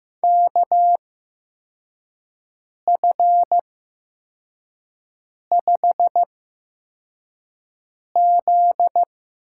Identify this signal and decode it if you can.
Morse code, heard as a keyed tone, about 15 wpm, spaced as for 4 wpm, 700 Hz. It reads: KF5Z
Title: USA Call Signs - 6 Characters - 8x Character Spacing - 15wpm